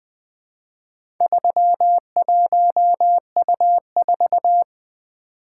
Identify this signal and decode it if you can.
Morse code, heard as a keyed tone, about 20 wpm, 700 Hz. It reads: 31U4
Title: Four Letter-Number Combinations - 20wpm